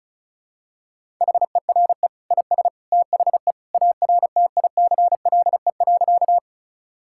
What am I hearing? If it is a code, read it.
Morse code, heard as a keyed tone, about 36 wpm, 700 Hz. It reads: HERE IS THE ARTICLE.
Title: Sentences from Top 1000 Words - Review Part 5 - 35wpm